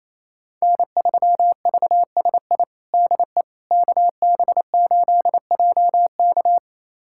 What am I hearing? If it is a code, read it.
Morse code, heard as a keyed tone, about 28 wpm, 700 Hz. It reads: N3VSI DE KB8JK
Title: USA Callsign DE Callsign - 5 Characters - 28wpm